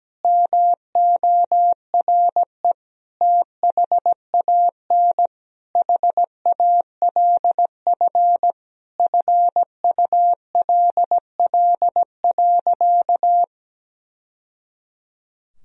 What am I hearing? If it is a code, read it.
Morse code, heard as a keyed tone, about 17 wpm, 700 Hz. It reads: MORE THAN HALF FULL.
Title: Sentences from Top 1000 Words - Review Part 3 - 17wpm